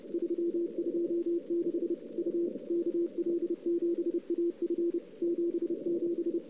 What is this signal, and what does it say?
Morse code, heard as a keyed tone, about 30 wpm, 340 Hz. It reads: 326 U KL7AF 77
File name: CW / 2008-ss-cw-k5tr-op-n5rz -03:00